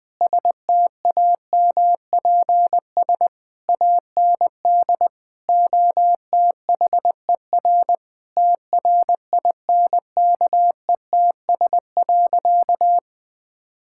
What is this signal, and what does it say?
Morse code, heard as a keyed tone, about 20 wpm, 700 Hz. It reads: STAMPS AND OTHER TRINKETS.
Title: Sentences with 2-Word Phrases - Spoken Beginning - 20wpm